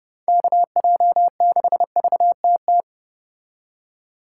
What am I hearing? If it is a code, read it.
Morse code, heard as a keyed tone, about 31 wpm, 700 Hz. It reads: KJ6VTT